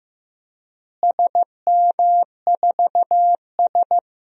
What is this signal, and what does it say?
Morse code, heard as a keyed tone, about 15 wpm, 700 Hz. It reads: SM4S
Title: DXCC-100 Call Signs - 4 Characters - Repeated 3x - 15wpm